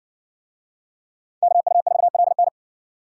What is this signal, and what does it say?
Morse code, heard as a keyed tone, about 53 wpm, 700 Hz. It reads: KU4LN